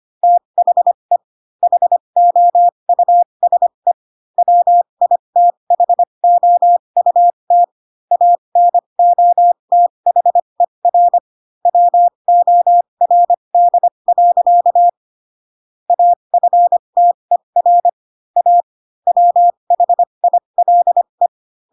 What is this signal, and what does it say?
Morse code, heard as a keyed tone, about 25 wpm, spaced as for 22 wpm, 700 Hz. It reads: THE HOUSE WITHOUT ANOTHER WORD. AFTER A WHILE